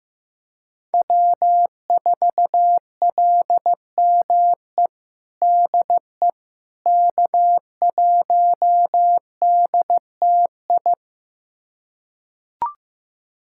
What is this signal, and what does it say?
Morse code, heard as a keyed tone, about 15 wpm, 700 Hz. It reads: W4LME DE K1DTI
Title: USA Callsign DE Callsign - 5 Characters - 15wpm